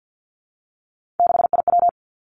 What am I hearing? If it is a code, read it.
Morse code, heard as a keyed tone, about 50 wpm, 700 Hz. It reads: 6IW
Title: Three Challenging Letter-Number Combinations 50wpm